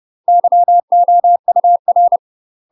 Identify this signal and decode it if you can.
Morse code, heard as a keyed tone, about 30 wpm, 700 Hz. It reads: YOUR
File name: UK MASTER QSOs TRAINING/30WPM